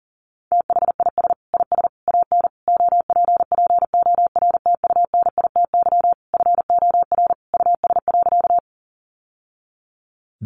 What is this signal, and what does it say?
Morse code, heard as a keyed tone, about 40 wpm, 700 Hz. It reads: THIS IS AN OPPORTUNITY FOR US.
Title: Sentences from Top 1000 Words 40wpm